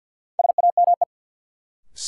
Morse code, heard as a keyed tone, about 53 wpm, 700 Hz. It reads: SAME
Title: Sets of 2 Words - New Word Drill - 50wpm